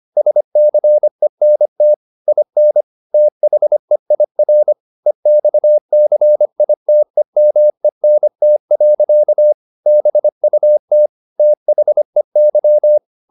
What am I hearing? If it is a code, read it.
Morse code, heard as a keyed tone, about 25 wpm, 600 Hz. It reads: SCENT IN THEIR EXCITEMENT. BUT THEY